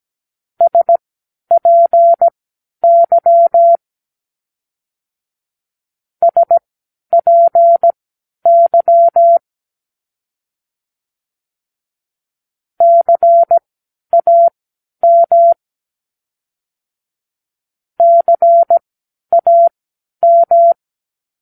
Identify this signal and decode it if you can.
Morse code, heard as a keyed tone, about 17 wpm, spaced as for 10 wpm, 680 Hz. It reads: SPY SPY CAM CAM
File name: k7qo / Practicewords